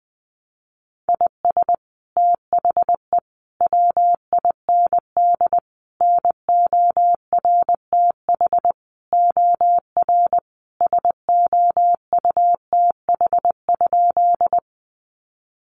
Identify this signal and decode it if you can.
Morse code, heard as a keyed tone, about 20 wpm, 700 Hz. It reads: IS THE WIND NORTH OR SOUTH?
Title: Sentences from Top 400 Words 20wpm